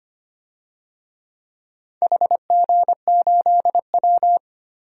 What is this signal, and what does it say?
Morse code, heard as a keyed tone, about 25 wpm, 700 Hz. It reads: HG8W
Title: DXCC-100 Call Signs - 4 Characters - 25wpm